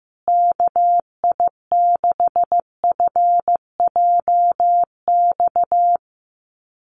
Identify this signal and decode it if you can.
Morse code, heard as a keyed tone, about 15 wpm, 700 Hz. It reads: KI6FJX
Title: USA Call Signs Repeated 3x 15wpm